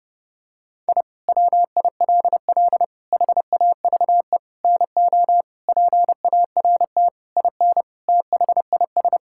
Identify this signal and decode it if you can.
Morse code, heard as a keyed tone, about 30 wpm, 700 Hz. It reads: I WILL HAVE NO PART IN THIS
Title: Sentences from Top 1000 Words - Review Part 5 - 30wpm